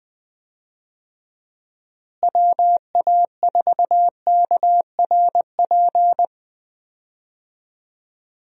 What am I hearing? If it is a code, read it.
Morse code, heard as a keyed tone, about 20 wpm, 700 Hz. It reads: WA4KRP
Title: USA Call Signs - 6 Characters - Repeated 3x - 20wpm